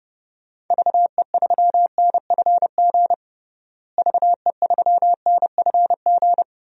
Morse code, heard as a keyed tone, about 30 wpm, 700 Hz. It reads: VE3NFG VE3NFG
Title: DXCC-100 Call Signs - 6 Characters - Repeated 2x - 30wpm